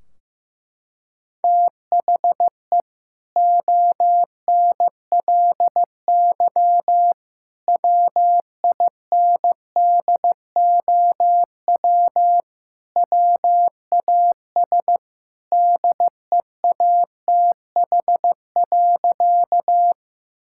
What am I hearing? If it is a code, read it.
Morse code, heard as a keyed tone, about 15 wpm, 700 Hz. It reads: THE ONLY WINDOW WAS DEATH.
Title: Sentences from Top 1600 Words 15wpm